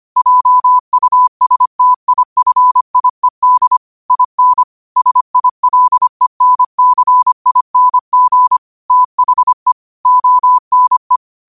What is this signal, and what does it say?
Morse code, heard as a keyed tone, about 25 wpm, 1000 Hz. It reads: JUSTIFIED IN SILENCING THE ONE
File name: qotdfiles / 25wpm